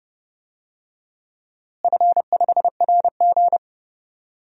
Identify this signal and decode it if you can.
Morse code, heard as a keyed tone, about 30 wpm, 700 Hz. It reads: F5RG
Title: DXCC-100 Call Signs - 4 Characters - Repeated 3x - 30wpm